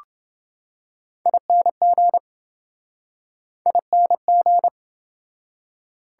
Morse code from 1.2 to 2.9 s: ING

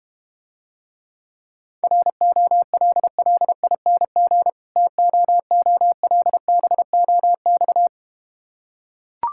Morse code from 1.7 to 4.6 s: ROLLING

